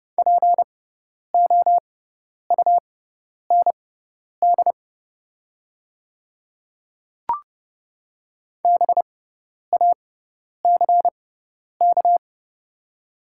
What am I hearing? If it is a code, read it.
Morse code, heard as a keyed tone, about 29 wpm, spaced as for 10 wpm, 700 Hz. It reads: POUND BACK
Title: 6x Character Spacing - Top 500 Words - 30wpm